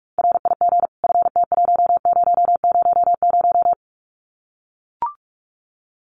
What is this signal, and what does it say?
Morse code, heard as a keyed tone, about 44 wpm, 700 Hz. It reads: RIG FT1000
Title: Common QSO Phrases 45wpm